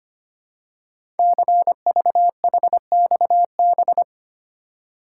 Morse code, heard as a keyed tone, about 25 wpm, 700 Hz. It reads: CVHXB
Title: Code Groups - 5 Letters - 25wpm